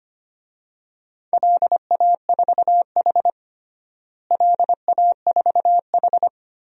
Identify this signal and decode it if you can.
Morse code, heard as a keyed tone, about 25 wpm, 700 Hz. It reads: LA4H LA4H
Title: DXCC-100 Call Signs - 4 Characters - Repeated 2x - 25wpm